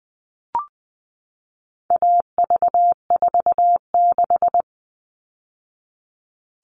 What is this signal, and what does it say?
Morse code, heard as a keyed tone, about 20 wpm, 700 Hz. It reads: AV46
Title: All Cut Numbers - 4-Digit 20wpm